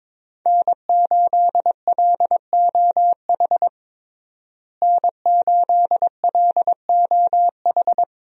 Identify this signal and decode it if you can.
Morse code, heard as a keyed tone, about 22 wpm, 700 Hz. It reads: N8LOH N8LOH
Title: USA Call Signs - 5 Characters - Repeated 2x - 22wpm